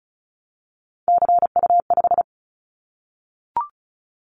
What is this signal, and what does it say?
Morse code, heard as a keyed tone, about 34 wpm, 700 Hz. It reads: CU5
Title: Callsign Prefixes for Top 50 DXCC - 35wpm